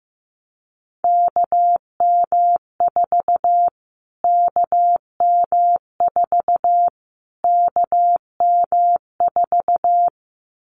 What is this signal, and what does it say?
Morse code, heard as a keyed tone, about 15 wpm, 700 Hz. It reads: KM4 KM4 KM4
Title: Callsign Prefixes for Top 25 DXCC - Repeated 3x - 15wpm